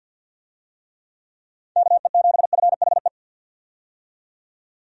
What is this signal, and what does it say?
Morse code, heard as a keyed tone, about 52 wpm, 700 Hz. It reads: KE7FHE
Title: USA Call Signs - 6 Characters - 50wpm